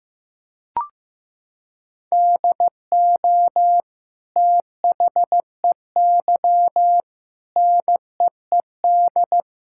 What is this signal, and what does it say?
Morse code, heard as a keyed tone, about 15 wpm, 700 Hz. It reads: DO THEY NEED
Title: Sentences from Top 1000 Words - Review Part 5 - 15wpm